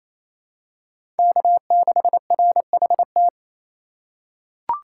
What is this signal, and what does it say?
Morse code, heard as a keyed tone, about 28 wpm, 700 Hz. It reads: K6RHT
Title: USA Call Signs - 5 Characters - 4x Character Spacing - 28wpm